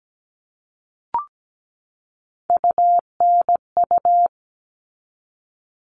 Morse code, heard as a keyed tone, about 17 wpm, 700 Hz. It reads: UNU
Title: All Cut Numbers - 3-Digit 17wpm